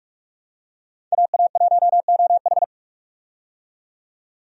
Morse code, heard as a keyed tone, about 43 wpm, 700 Hz. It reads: AA1OH
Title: USA Call Signs - 5 Characters - 45wpm